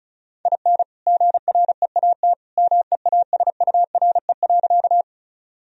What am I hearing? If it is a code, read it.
Morse code, heard as a keyed tone, about 34 wpm, 700 Hz. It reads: IN GREAT MEASURE.
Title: Sentences from Top 1000 Words - Review Part 2 - 35wpm